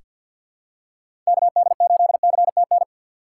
Morse code, heard as a keyed tone, about 46 wpm, 700 Hz. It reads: KD8CTN
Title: USA Call Signs - 3x Character Spacing - 50wpm